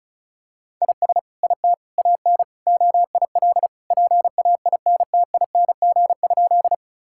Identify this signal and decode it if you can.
Morse code, heard as a keyed tone, about 34 wpm, 700 Hz. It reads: IS IT AN OIL PAINTING?
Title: Sentences from Top 1000 Words - Review Part 6 - 35wpm